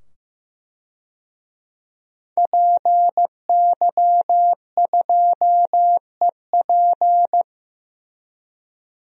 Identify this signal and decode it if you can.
Morse code, heard as a keyed tone, about 16 wpm, 700 Hz. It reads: PY2EP
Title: DXCC-100 Call Signs - 5 Characters - 15wpm